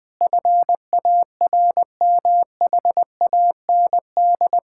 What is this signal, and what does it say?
Morse code, heard as a keyed tone, about 20 wpm, 700 Hz. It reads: FARMHAND
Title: Pangram Challenge v2 - 20wpm